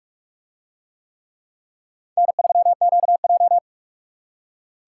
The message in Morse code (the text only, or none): N3QJ